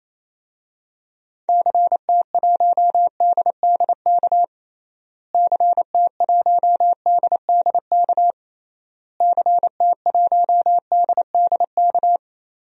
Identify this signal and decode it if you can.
Morse code, heard as a keyed tone, about 28 wpm, 700 Hz. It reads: CT1DDK CT1DDK CT1DDK
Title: DXCC-100 Call Signs - 6 Characters - Repeated 3x - 28wpm